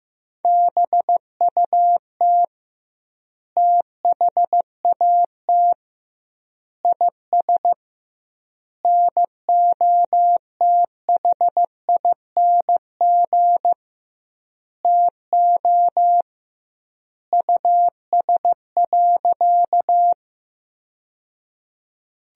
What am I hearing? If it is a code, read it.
Morse code, heard as a keyed tone, about 15 wpm, 700 Hz. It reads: BUT THAT IS NOTHING TO US.
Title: 2x Word-Spacing - Sentences from Top 500 Words - 15wpm